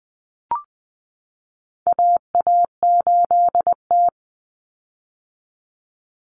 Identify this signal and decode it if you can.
Morse code, heard as a keyed tone, about 20 wpm, 700 Hz. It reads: AA8T
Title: Common Cut Numbers - 4-Digit 20wpm